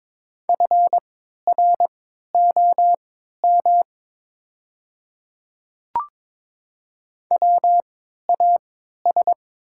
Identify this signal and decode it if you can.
Morse code, heard as a keyed tone, about 23 wpm, spaced as for 13 wpm, 700 Hz. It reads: FROM WAS
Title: 3x Character Spacing - Top 100 Words - 22wpm